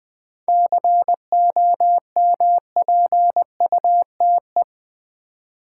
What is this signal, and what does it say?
Morse code, heard as a keyed tone, about 20 wpm, 700 Hz. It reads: COMPUTE
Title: 7 Letter Words - First 3 Letters Given - 20wpm